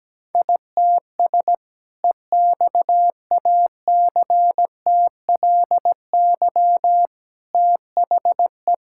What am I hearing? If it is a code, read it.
Morse code, heard as a keyed tone, about 17 wpm, 700 Hz. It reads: ITS EXACTLY THE
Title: Sentences from Top 1000 Words - Review Part 6 - 17wpm